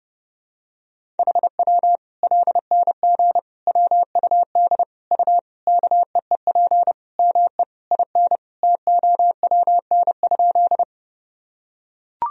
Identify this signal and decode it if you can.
Morse code, heard as a keyed tone, about 30 wpm, 700 Hz. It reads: HW LNG WUD U KEEP ME IN TOWN?
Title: Sentences from Top 500 Words and Abbreviations 30wpm